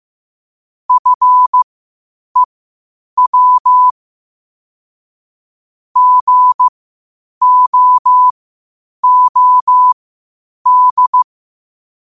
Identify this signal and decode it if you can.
Morse code, heard as a keyed tone, about 15 wpm, spaced as for 8 wpm, 1000 Hz. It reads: FEW GOOD
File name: qotdfiles / 8wpm